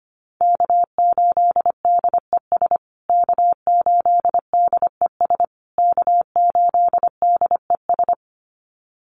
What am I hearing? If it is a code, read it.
Morse code, heard as a keyed tone, about 25 wpm, 700 Hz. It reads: K8DES K8DES K8DES
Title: USA Call Signs Repeated 3x 25wpm